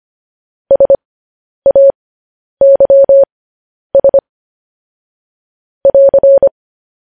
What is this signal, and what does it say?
Morse code, heard as a keyed tone, about 25 wpm, spaced as for 10 wpm, 550 Hz. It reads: SAYS <AR>
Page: news headlines in morse code at 10 wpm